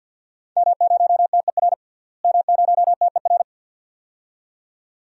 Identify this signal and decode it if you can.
Morse code, heard as a keyed tone, about 52 wpm, 700 Hz. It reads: M0TER M0TER
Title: DXCC-100 Call Signs - 5 Characters - Repeated 2x - 50wpm